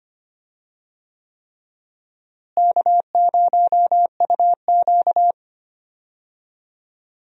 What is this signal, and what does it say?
Morse code, heard as a keyed tone, about 25 wpm, 700 Hz. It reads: K0UQ
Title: USA Call Signs - 4 Characters - Repeated 3x - 25wpm